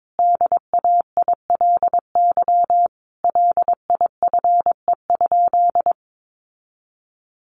Morse code, heard as a keyed tone, about 22 wpm, 700 Hz. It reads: DAILY LIFE?
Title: Sentences from Top 1100 Words 22wpm